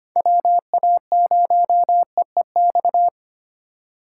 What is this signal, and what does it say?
Morse code, heard as a keyed tone, about 25 wpm, 700 Hz. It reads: WA0EEX
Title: USA Call Signs - 6 Characters - 3x Character Spacing - 25wpm